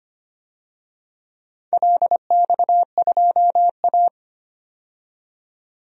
Morse code, heard as a keyed tone, about 25 wpm, 700 Hz. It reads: LX2A
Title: DXCC-100 Call Signs - 4 Characters - 25wpm